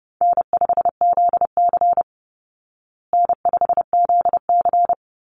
Morse code, heard as a keyed tone, about 30 wpm, 700 Hz. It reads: N5ZC N5ZC
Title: Easy Call Signs Repeated 2x - 30wpm